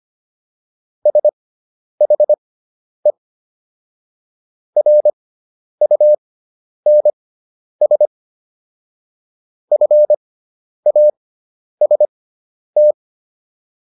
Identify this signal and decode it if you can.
Morse code, heard as a keyed tone, about 25 wpm, spaced as for 10 wpm, 600 Hz. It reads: SHE RUNS FAST